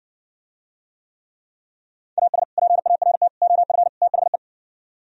Morse code, heard as a keyed tone, about 60 wpm, 700 Hz. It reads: AS PART OF THE